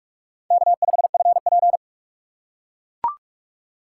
Morse code, heard as a keyed tone, about 44 wpm, 690 Hz. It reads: K5FP
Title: USA Call Signs - 4 Characters - 3x Character Spacing - 45wpm